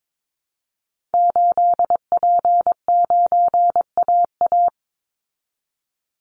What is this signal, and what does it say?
Morse code, heard as a keyed tone, about 22 wpm, 700 Hz. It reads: 8P9AA
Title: Call Signs Repeated 3x - 22wpm